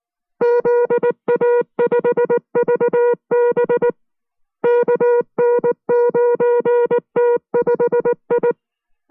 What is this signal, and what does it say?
Morse code, heard as a keyed tone, about 19 wpm, 480 Hz. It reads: ZA5VB KN9T5I